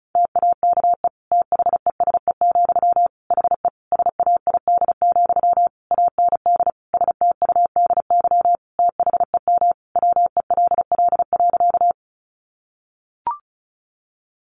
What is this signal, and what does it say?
Morse code, heard as a keyed tone, about 34 wpm, 700 Hz. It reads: TAKE THESE, HE SAID, AND STUDY THEM WELL.
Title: Sentences from Top 300 Words 35wpm